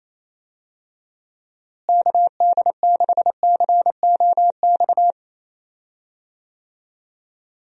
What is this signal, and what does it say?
Morse code, heard as a keyed tone, about 28 wpm, 700 Hz. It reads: KD6COX
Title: USA Call Signs - 6 Characters - Repeated 2x - 28wpm